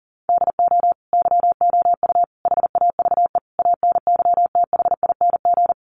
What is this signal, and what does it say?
Morse code, heard as a keyed tone, about 40 wpm, 700 Hz. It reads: DO YOU HAVE ANYTHING
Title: Sentences from Top 1000 Words 40wpm